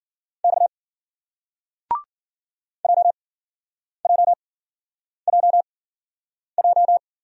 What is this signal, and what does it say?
Morse code, heard as a keyed tone, about 50 wpm, spaced as for 9 wpm, 700 Hz. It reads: X JJJJ